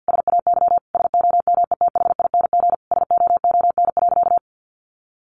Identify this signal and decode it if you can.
Morse code, heard as a keyed tone, about 50 wpm, 700 Hz. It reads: SAY SOMETHING SOON.
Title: Sentences from Top 1500 Words 50wpm